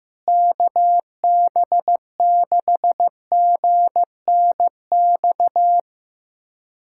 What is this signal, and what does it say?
Morse code, heard as a keyed tone, about 15 wpm, 700 Hz. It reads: KB6GNX
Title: USA Callsign DE Callsign 15wpm